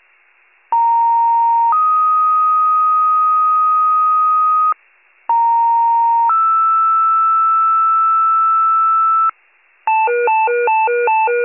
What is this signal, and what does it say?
Two-tone page, heard as a one-second tone, then a three-second tone, about 930 then 1250 Hz, about 930 then 1320 Hz. It is heard from 0.7 to 9.3 s.